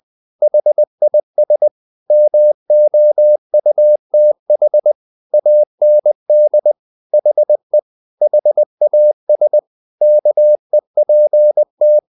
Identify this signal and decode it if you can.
Morse code, heard as a keyed tone, about 20 wpm, 600 Hz. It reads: HIS MOUTH AND HE HAS KEPT
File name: the old man and the sea part 2 20wpm